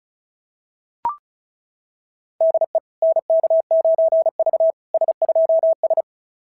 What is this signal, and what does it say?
Morse code, heard as a keyed tone, about 36 wpm, 690 Hz. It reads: DE NK9V S2S
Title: SOTA QSO Practice 35wpm